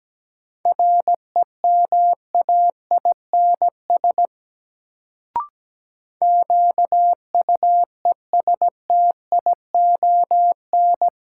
Morse code, heard as a keyed tone, about 17 wpm, 700 Hz. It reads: REMAINS QUESTION
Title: Sets of 5 Words Encore - New Word Drill - 17wpm